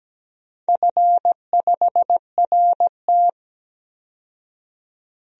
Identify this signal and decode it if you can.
Morse code, heard as a keyed tone, about 17 wpm, 700 Hz. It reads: F5RT
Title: DXCC-100 Call Signs - 4 Characters - 17wpm